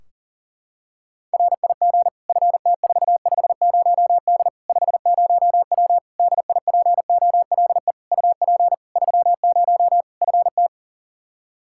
Morse code, heard as a keyed tone, about 40 wpm, 700 Hz. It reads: RIG FT450D 50W DIPOLE UP 30 FT
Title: Practice QSO - Exchange 2 - Info - 40wpm